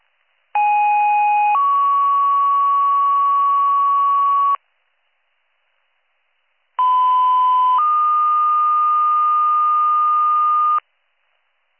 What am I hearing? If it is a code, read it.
Two-tone page, heard as a one-second tone, then a three-second tone, about 830 then 1120 Hz, about 1010 then 1220 Hz. It is heard from 0.5 to 10.8 s.